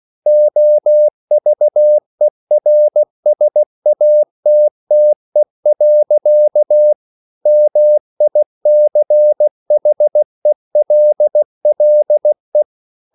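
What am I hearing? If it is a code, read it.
Morse code, heard as a keyed tone, about 16 wpm, 600 Hz. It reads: OVERSATTE. MICHELLE
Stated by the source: Slow (QRS) Morse code at 16 wpm, from CW med Gnister.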